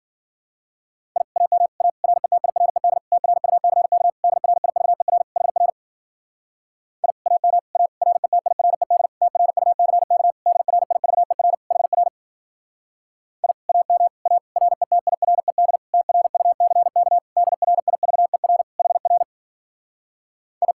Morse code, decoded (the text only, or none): I AM A RETIRED TRUCK DRIVER HR I AM A RETIRED TRUCK DRIVER HR I AM A RETIRED TRUCK DRIVER HR I